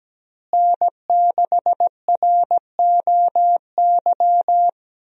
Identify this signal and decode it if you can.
Morse code, heard as a keyed tone, about 17 wpm, 700 Hz. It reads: N6ROY
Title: USA Call Signs - 5 Characters - Repeated 3x - 17wpm